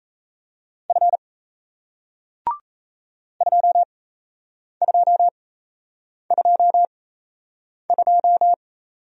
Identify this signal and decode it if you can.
Morse code, heard as a keyed tone, about 37 wpm, spaced as for 8 wpm, 700 Hz. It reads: F 2222